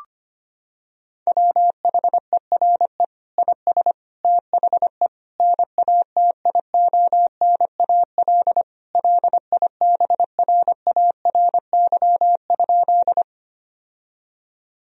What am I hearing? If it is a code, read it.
Morse code, heard as a keyed tone, about 25 wpm, 700 Hz. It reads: WHERE IS THE NATIONAL LIBRARY?
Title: Sentences from Top 1000 Words - Review Part 4 - 25wpm